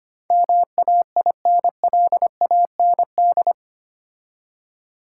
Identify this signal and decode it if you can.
Morse code, heard as a keyed tone, about 25 wpm, 700 Hz. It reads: MAINLAND
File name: 8 Letter Words - First 3 Letters Given - 25wpm